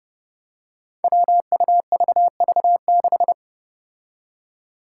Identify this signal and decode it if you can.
Morse code, heard as a keyed tone, about 30 wpm, 700 Hz. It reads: WUVV6
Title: Five Challenging Letter-Number Combinations 30wpm